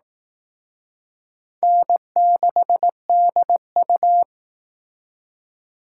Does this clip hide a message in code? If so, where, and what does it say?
Yes: Morse code from 1.6 to 4.2 s, heard as a keyed tone, about 19 wpm, 700 Hz. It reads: N6DU